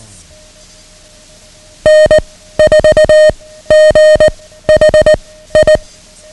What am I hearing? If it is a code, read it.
Morse code, heard as a keyed tone, about 19 wpm, 620 Hz. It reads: N 4 G H I
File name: ham radio / CodeQuick / Code Quick 09